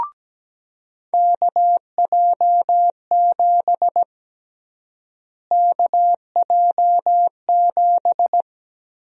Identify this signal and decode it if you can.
Morse code, heard as a keyed tone, about 17 wpm, 700 Hz. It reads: KJ7 KJ7
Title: USA Call Sign Prefixes - Repeated 2x - 17wpm